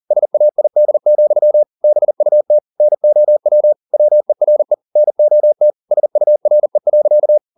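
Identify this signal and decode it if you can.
Morse code, heard as a keyed tone, about 40 wpm, 600 Hz. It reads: SAID, BUT NOW WERE NOT SURE.
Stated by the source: Morse code at 40 wpm, practice text, CW med Gnister.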